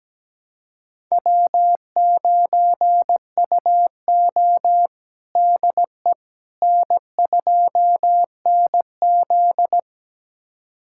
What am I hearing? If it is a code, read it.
Morse code, heard as a keyed tone, about 17 wpm, 700 Hz. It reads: W9UO DE N2NZ
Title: USA Callsign DE Callsign - 4 Characters - 17wpm